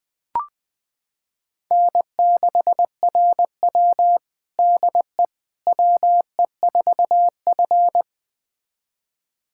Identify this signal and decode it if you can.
Morse code, heard as a keyed tone, about 20 wpm, 700 Hz. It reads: N6RW DE WE4F
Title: USA Callsign DE Callsign - 4 Characters - 20wpm